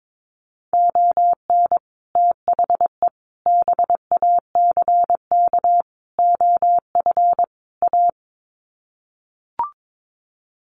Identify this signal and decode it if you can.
Morse code, heard as a keyed tone, about 22 wpm, 700 Hz. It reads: ON THE BACK OF A
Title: Sets of 5 Words Encore2 22wpm